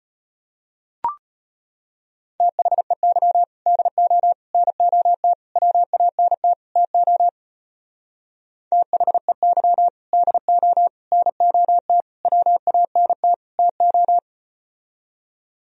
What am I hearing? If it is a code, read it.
Morse code, heard as a keyed tone, about 35 wpm, 700 Hz. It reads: THEY DO NOT WANT TO THEY DO NOT WANT TO